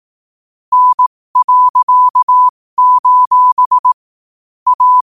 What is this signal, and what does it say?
Morse code, heard as a keyed tone, about 18 wpm, spaced as for 15 wpm, 1000 Hz. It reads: N.: A
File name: qotdfiles / 15wpm